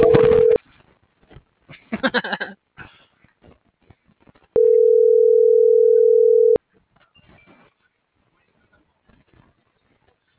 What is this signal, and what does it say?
Ringback tone (North America), 440 + 480 Hz.